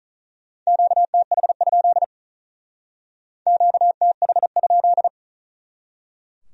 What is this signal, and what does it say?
Morse code, heard as a keyed tone, about 45 wpm, 700 Hz. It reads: QTH? QTH?